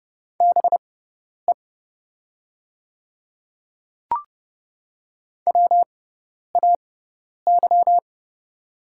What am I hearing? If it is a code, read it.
Morse code, heard as a keyed tone, about 30 wpm, spaced as for 10 wpm, 700 Hz. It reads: BE WAY
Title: 6x Character Spacing - Top 200 Words - 30wpm